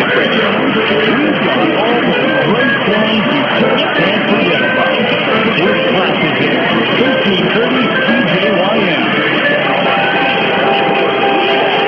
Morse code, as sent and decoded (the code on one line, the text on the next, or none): none